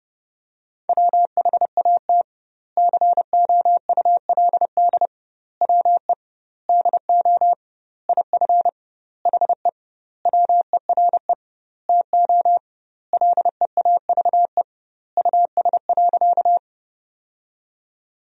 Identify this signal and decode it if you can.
Morse code, heard as a keyed tone, about 30 wpm, 700 Hz. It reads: WHAT COULD WE DO IF HE WERE TO LEAVE US.
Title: 2x Word-Spacing - Sentences from Top 400 Words - 30wpm